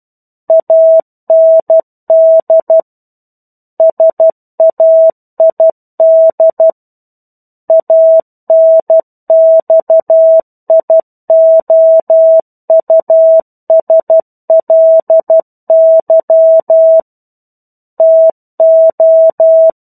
Morse code, heard as a keyed tone, about 12 wpm, 650 Hz. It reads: AND SAID ANXIOUSLY TO
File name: 01-Chapter-01 12 WPM